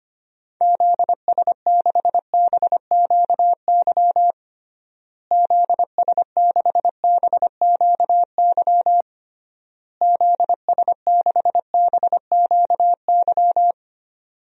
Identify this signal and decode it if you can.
Morse code, heard as a keyed tone, about 25 wpm, 700 Hz. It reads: ZS6BQY ZS6BQY ZS6BQY
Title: DXCC-100 Call Signs - 6 Characters - Repeated 3x - 25wpm